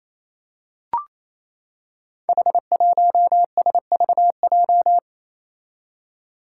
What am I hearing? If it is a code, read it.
Morse code, heard as a keyed tone, about 27 wpm, 700 Hz. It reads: H1SVJ
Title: Five Challenging Letter-Number Combinations 28wpm